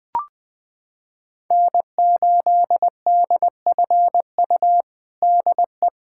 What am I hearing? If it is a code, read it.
Morse code, heard as a keyed tone, about 20 wpm, 700 Hz. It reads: N8DFU DE
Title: USA Callsign DE Callsign - 5 Characters - 20wpm